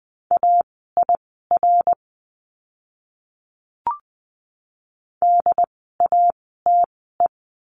Morse code, heard as a keyed tone, about 20 wpm, spaced as for 15 wpm, 700 Hz. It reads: AIR DATE
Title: 068 Lesson 12f - Letter d - Words 20wpm